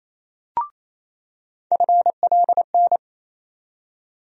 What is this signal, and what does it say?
Morse code, heard as a keyed tone, about 28 wpm, 700 Hz. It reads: FLN
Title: USA Call Sign Postfixes - 3 Letters 28wpm